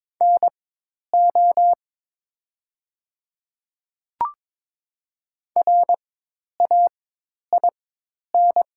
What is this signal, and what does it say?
Morse code, heard as a keyed tone, about 22 wpm, spaced as for 10 wpm, 700 Hz. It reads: NO RAIN
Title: 4x Character Spacing - Top 300 Words - 22wpm